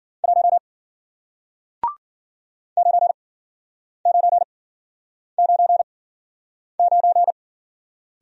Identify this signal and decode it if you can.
Morse code, heard as a keyed tone, about 53 wpm, spaced as for 9 wpm, 700 Hz. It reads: 1 9999